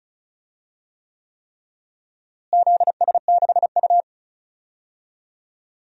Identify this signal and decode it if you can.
Morse code, heard as a keyed tone, about 34 wpm, 700 Hz. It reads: ZS6U
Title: DXCC-100 Call Signs - 4 Characters - Repeated 3x - 35wpm